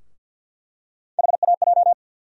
Morse code, heard as a keyed tone, about 50 wpm, 700 Hz. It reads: HAJ
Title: USA Call Sign Postfixes - 3 Letters 50wpm